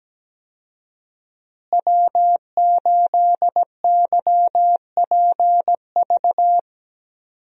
Morse code, heard as a keyed tone, about 17 wpm, 700 Hz. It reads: W8YPV
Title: USA Call Signs - 5 Characters - Repeated 2x - 17wpm